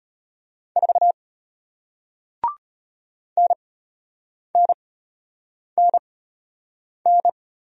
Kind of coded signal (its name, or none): Morse code